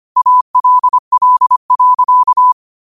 Morse code, heard as a keyed tone, about 25 wpm, 1000 Hz. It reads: ALL.